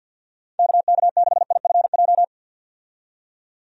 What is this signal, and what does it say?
Morse code, heard as a keyed tone, about 51 wpm, 700 Hz. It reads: KK6IFJ
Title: USA Call Signs - 6 Characters - 50wpm